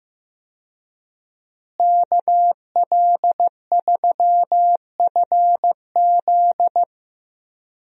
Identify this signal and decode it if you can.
Morse code, heard as a keyed tone, about 15 wpm, 700 Hz. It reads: KL3FZ
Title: USA Call Signs - 5 Characters - Repeated 3x - 15wpm